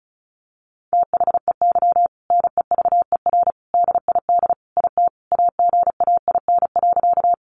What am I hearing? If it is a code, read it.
Morse code, heard as a keyed tone, about 34 wpm, 700 Hz. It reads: THEY NEVER DID IT AGAIN.
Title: Sentences from Top 300 Words 35wpm